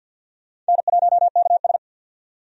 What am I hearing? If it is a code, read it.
Morse code, heard as a keyed tone, about 47 wpm, 700 Hz. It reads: N1KS
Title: USA Call Signs - 4 Characters - 50wpm